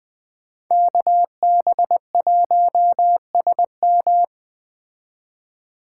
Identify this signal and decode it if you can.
Morse code, heard as a keyed tone, about 20 wpm, 700 Hz. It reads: KB1SM
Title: Speed Racing - USA Call Signs 17wpm